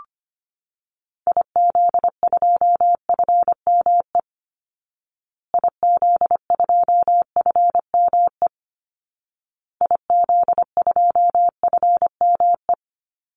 Morse code, heard as a keyed tone, about 25 wpm, 700 Hz. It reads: IZ2FME IZ2FME IZ2FME
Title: Call Signs Repeated 2x - 25wpm